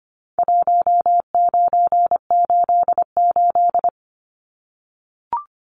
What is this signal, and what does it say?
Morse code, heard as a keyed tone, about 25 wpm, 700 Hz. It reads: 1988
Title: Numbers - Contest Style - 4-Digit 25wpm